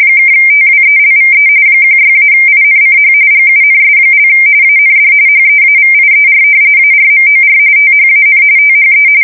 RTTY, about 45.45 baud. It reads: (TWELVE 16-INCH GUNS) AND HEAVIER ARMOR THAN THE
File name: fsk170 / battleships